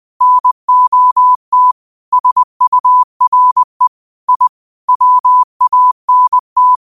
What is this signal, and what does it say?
Morse code, heard as a keyed tone, about 20 wpm, 1000 Hz. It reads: NOT SURE I WANT
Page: New quotes every day in morse code at 20 Words per minute.